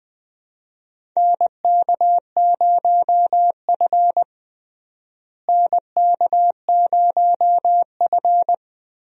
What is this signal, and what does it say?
Morse code, heard as a keyed tone, about 21 wpm, 700 Hz. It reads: NK0F NK0F
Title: USA Call Signs - 4 Characters - Repeated 3x - 20wpm